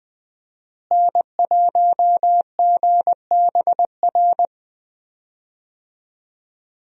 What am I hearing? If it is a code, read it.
Morse code, heard as a keyed tone, about 20 wpm, 700 Hz. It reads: N1GBR
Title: USA Call Signs - 5 Characters - Repeated 3x - 20wpm